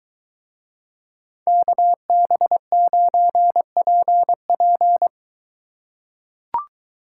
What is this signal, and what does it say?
Morse code, heard as a keyed tone, about 23 wpm, 700 Hz. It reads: KB9PP